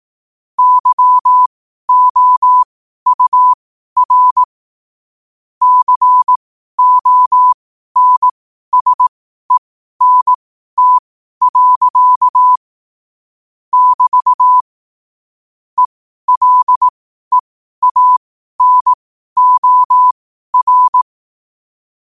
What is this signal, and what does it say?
Morse code, heard as a keyed tone, about 18 wpm, spaced as for 12 wpm, 1000 Hz. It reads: YOUR CONSENT. <BT> ELEANOR